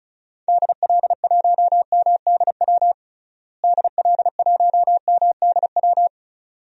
Morse code, heard as a keyed tone, about 34 wpm, 700 Hz. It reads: DL1MDW DL1MDW
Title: DXCC-100 Call Signs - 6 Characters - Repeated 2x - 35wpm